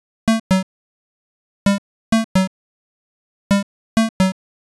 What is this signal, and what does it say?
There is no code.